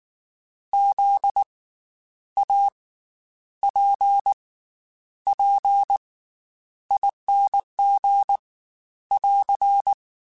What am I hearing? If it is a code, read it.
Morse code, heard as a keyed tone, about 19 wpm, spaced as for 9 wpm, 800 Hz. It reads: ZAPPING <AR>
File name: Special INGs / Special INGs-mp3s / Slow ing @18